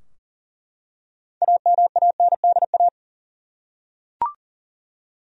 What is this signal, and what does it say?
Morse code, heard as a keyed tone, about 40 wpm, 700 Hz. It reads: AMANDA
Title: USA Common Names - 40wpm